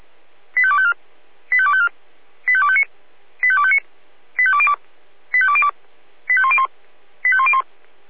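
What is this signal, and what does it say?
Five-tone selective call (ZVEI), heard as quick runs of five tones, tones of about 70 ms. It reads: 85254 85254 85259 85259 85202 85202 85101 85101
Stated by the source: ZVEI five-tone selective call received on an SDR.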